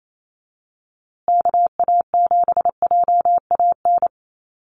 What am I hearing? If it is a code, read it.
Morse code, heard as a keyed tone, about 28 wpm, 700 Hz. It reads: KA7JAN
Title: USA Call Signs Repeated 2x 28wpm